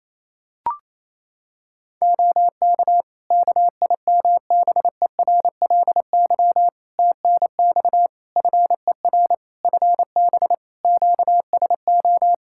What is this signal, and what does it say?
Morse code, heard as a keyed tone, about 27 wpm, 700 Hz. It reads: OK KIMBERLY TNX FER FB QSO
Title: Practice QSO - Exchange 4 - Ending - 28wpm